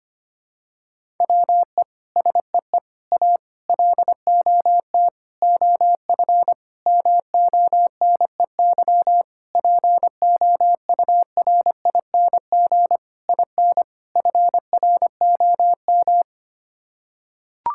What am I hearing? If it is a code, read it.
Morse code, heard as a keyed tone, about 25 wpm, 700 Hz. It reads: WE SEE A LOT OF MONEY POURING IN FROM
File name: Sets of 9 Words - 25wpm